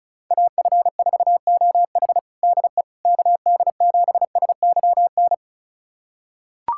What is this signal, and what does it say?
Morse code, heard as a keyed tone, about 36 wpm, 700 Hz. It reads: AF4OH DE KD7SYN
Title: USA Callsign DE Callsign 35wpm